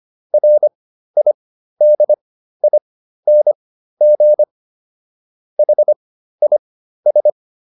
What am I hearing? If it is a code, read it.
Morse code, heard as a keyed tone, about 25 wpm, spaced as for 13 wpm, 600 Hz. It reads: RIDING HIS